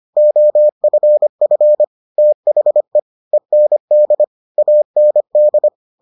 Morse code, heard as a keyed tone, about 25 wpm, 600 Hz. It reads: OFF THE END AND